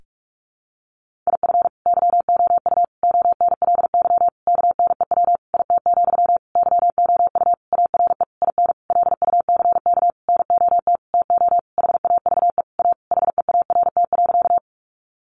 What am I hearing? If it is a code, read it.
Morse code, heard as a keyed tone, about 44 wpm, 700 Hz. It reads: IF YOU ONLY KNEW IT, YOU ARE IN LUCK NOT TO HAVE A HEART.
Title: Sentences from Top 1500 Words 45wpm